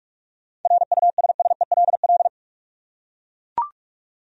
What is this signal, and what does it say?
Morse code, heard as a keyed tone, about 44 wpm, 700 Hz. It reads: RUSSELL
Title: USA Common Names - 45wpm